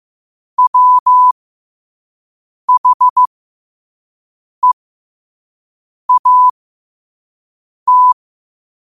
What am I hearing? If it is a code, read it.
Morse code, heard as a keyed tone, about 15 wpm, spaced as for 5 wpm, 1000 Hz. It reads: WHEAT
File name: qotdfiles / 5wpm